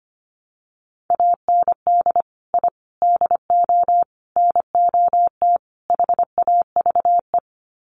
Morse code, heard as a keyed tone, about 25 wpm, 700 Hz. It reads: AND I DO NOT HAVE
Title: Sets of 5 Words Encore 25wpm